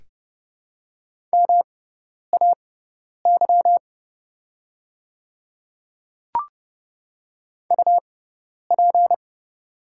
Morse code, heard as a keyed tone, about 30 wpm, spaced as for 10 wpm, 700 Hz. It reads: MAY UP
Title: 6x Character Spacing - Top 100 Words - 30wpm